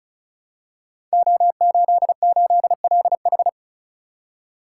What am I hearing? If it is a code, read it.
Morse code, heard as a keyed tone, about 36 wpm, 700 Hz. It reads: O88LH
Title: Code Groups - 5 Letter:Numbers - 35wpm